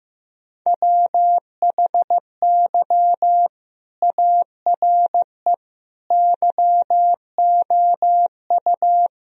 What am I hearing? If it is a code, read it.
Morse code, heard as a keyed tone, about 15 wpm, 700 Hz. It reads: WHY ARE YOU
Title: Sentences from Top 1000 Words - Review Part 4 - 15wpm